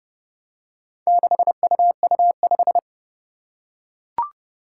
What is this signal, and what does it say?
Morse code, heard as a keyed tone, about 31 wpm, 700 Hz. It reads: 6UU5
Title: Four Challenging Letter-Number Combinations 30wpm